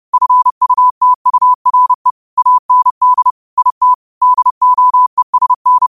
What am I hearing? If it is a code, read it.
Morse code, heard as a keyed tone, about 30 wpm, 1000 Hz. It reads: FUTURE AND IT DOESN
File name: qotdfiles / 30wpm